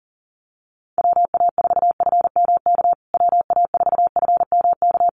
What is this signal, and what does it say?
Morse code, heard as a keyed tone, about 40 wpm, 700 Hz. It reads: WA4FMK WA4FMK
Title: USA Call Signs Repeated 3x 40wpm